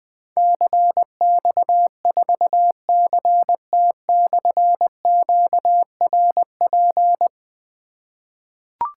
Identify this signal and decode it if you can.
Morse code, heard as a keyed tone, about 20 wpm, 700 Hz. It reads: CX4CT/QRP
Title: Difficult Call Signs 20wpm